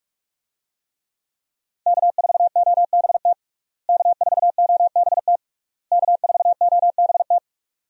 Morse code, heard as a keyed tone, about 42 wpm, 700 Hz. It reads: K4OBT K4OBT K4OBT
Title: USA Call Signs - 5 Characters - Repeated 3x - 45wpm